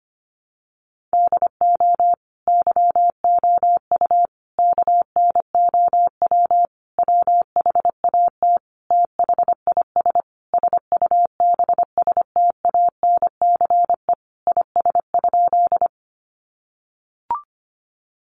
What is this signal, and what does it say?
Morse code, heard as a keyed tone, about 25 wpm, 700 Hz. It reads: DO YOU KNOW WHAT THIS SUBSTANCE IS?
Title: Sentences from Top 1500 Words 25wpm